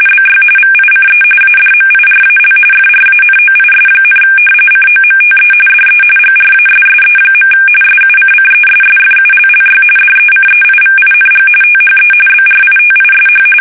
RTTY, about 45.45 baud. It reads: BY VICE ADMIRAL FRANOIS, COMTE DE GRASSE. SUCCESSFULLY DECEIVING THE BRITISH
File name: fsk850 / usnavy1